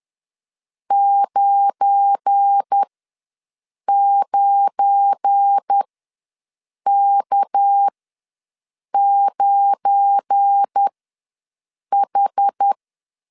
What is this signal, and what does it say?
Morse code, heard as a keyed tone, about 10 wpm, spaced as for 6 wpm, 800 Hz. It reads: 99K9H
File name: LEZIONI / 7 GF9Z / 7 GF9 ws 10wpm 4034